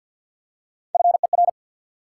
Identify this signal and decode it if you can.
Morse code, heard as a keyed tone, about 48 wpm, 700 Hz. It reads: FER
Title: Top 180 QSO Elements 50wpm